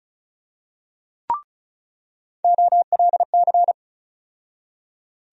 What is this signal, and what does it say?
Morse code, heard as a keyed tone, about 36 wpm, 700 Hz. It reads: OLC
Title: USA Call Sign Postfixes - 3 Letters 35wpm